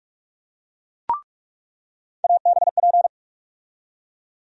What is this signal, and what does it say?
Morse code, heard as a keyed tone, about 42 wpm, 700 Hz. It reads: ABP
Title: USA Call Sign Postfixes - 3 Letters 45wpm